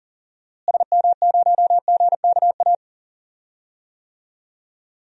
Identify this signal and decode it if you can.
Morse code, heard as a keyed tone, about 40 wpm, 700 Hz. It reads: SM0GKA
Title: DXCC-100 Call Signs - 6 Characters - Repeated 3x - 40wpm